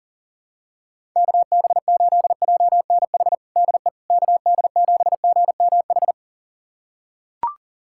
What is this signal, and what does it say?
Morse code, heard as a keyed tone, about 37 wpm, 700 Hz. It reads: KB8JNH DE KD7GMH
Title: USA Callsign DE Callsign - 6 Characters - 40wpm